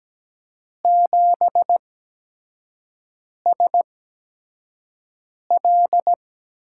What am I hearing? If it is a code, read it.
Morse code, heard as a keyed tone, about 17 wpm, spaced as for 5 wpm, 700 Hz. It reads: 7SL
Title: USA Call Signs - 8x Character Spacing - 17wpm